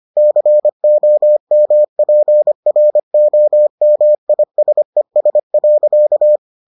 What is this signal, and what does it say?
Morse code, heard as a keyed tone, about 25 wpm, 600 Hz. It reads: COMPROMISES.